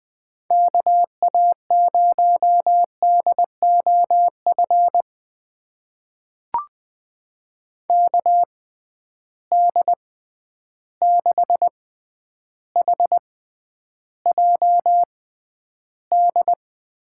Morse code, heard as a keyed tone, about 20 wpm, spaced as for 10 wpm, 700 Hz. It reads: KA0DOF KD6HJD
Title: USA Call Signs - 6 Characters - 6x Character Spacing - 20wpm